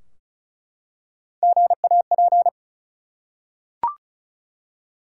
Morse code, heard as a keyed tone, about 36 wpm, 700 Hz. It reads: GAP
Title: Three Letter Words - 35wpm